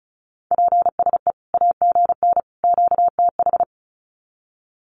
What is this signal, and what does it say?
Morse code, heard as a keyed tone, about 34 wpm, 700 Hz. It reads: PSE AGN QTH
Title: Common QSO Phrases 35wpm